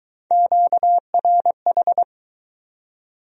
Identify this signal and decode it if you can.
Morse code, heard as a keyed tone, about 22 wpm, 700 Hz. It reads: QRH